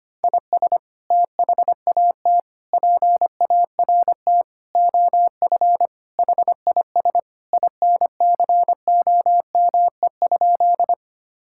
Morse code, heard as a keyed tone, about 25 wpm, 700 Hz. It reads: IS THAT PART OF HIS INCOME?
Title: Sentences from Top 1000 Words - Review Part 5 - 25wpm